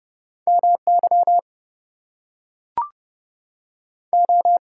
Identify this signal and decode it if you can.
Morse code, heard as a keyed tone, about 30 wpm, 700 Hz. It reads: MY O
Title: Two Letter Words - 30wpm